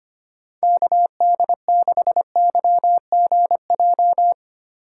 Morse code, heard as a keyed tone, about 25 wpm, 700 Hz. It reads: KD6YGJ
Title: USA Call Signs - 6 Characters - 4x Character Spacing - 25wpm